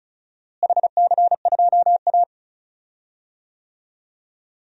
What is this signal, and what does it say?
Morse code, heard as a keyed tone, about 36 wpm, 700 Hz. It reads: HC2A
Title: DXCC-100 Call Signs - 4 Characters - Repeated 3x - 35wpm